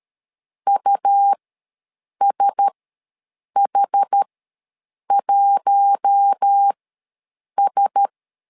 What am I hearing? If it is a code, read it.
Morse code, heard as a keyed tone, about 13 wpm, spaced as for 7 wpm, 800 Hz. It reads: USH1S
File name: LEZIONI / 4 Q1S ws 12wpm 1827